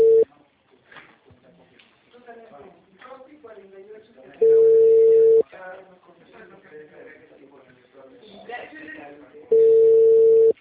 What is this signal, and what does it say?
Ringback tone (China, 450 Hz), with speech.